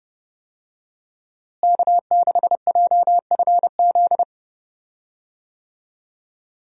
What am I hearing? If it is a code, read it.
Morse code, heard as a keyed tone, about 29 wpm, 700 Hz. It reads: K6JFZ
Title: USA Call Signs - 5 Characters - Repeated 2x - 30wpm